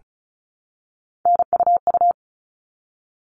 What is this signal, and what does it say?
Morse code, heard as a keyed tone, about 34 wpm, 700 Hz. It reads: NUU
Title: All Cut Numbers - 3-Digit 35wpm